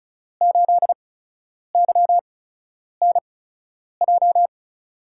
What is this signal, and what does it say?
Morse code, heard as a keyed tone, about 36 wpm, spaced as for 10 wpm, 700 Hz. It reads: 8YNJ
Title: USA Call Signs - 6 Characters - 8x Character Spacing - 35wpm